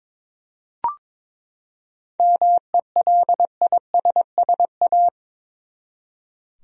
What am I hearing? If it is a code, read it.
Morse code, heard as a keyed tone, about 22 wpm, 700 Hz. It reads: MELISSA
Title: USA Common Names - 22wpm